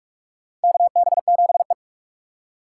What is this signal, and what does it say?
Morse code, heard as a keyed tone, about 43 wpm, 700 Hz. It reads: KB7E
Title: USA Call Signs - 4 Characters - 6x Character Spacing - 45wpm